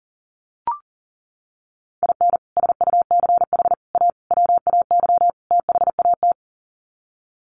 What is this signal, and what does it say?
Morse code, heard as a keyed tone, about 40 wpm, 700 Hz. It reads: IN SUCH A WAY THAT
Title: Sets of 5 Words 40wpm